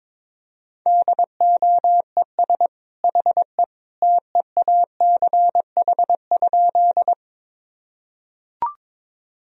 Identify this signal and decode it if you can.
Morse code, heard as a keyed tone, about 22 wpm, 700 Hz. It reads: DOES HE TEACH?
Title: Sentences from Top 1000 Words - Review Part 2 - 22wpm